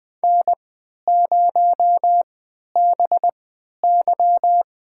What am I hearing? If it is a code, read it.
Morse code, heard as a keyed tone, about 20 wpm, spaced as for 11 wpm, 700 Hz. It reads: N0BY
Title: USA Call Signs - 4 Characters - 3x Character Spacing - 20wpm